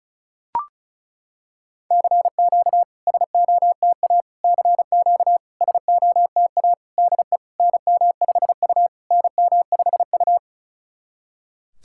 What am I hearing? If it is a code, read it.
Morse code, heard as a keyed tone, about 34 wpm, 690 Hz. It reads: CQ SOTA CQ SOTA DE NM5U NM5U
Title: SOTA QSO Practice 35wpm